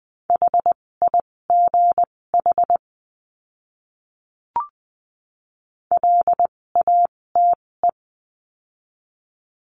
Morse code, heard as a keyed tone, about 20 wpm, spaced as for 16 wpm, 700 Hz. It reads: HIGH LATE
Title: 256 Lesson 42a - Words 12-20wpm